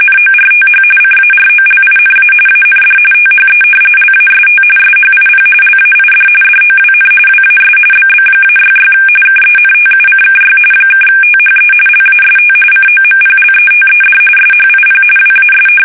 RTTY, about 45.45 baud. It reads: NORTH CAROLINA AND THE LAST SHIP LAID DOWN WAS BB-66 KENTUCKY (THE LAST COMPLETED SHIP WAS
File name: fsk850 / battleships